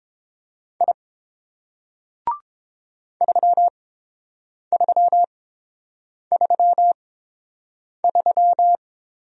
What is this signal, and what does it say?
Morse code, heard as a keyed tone, about 30 wpm, spaced as for 7 wpm, 690 Hz. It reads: I 3333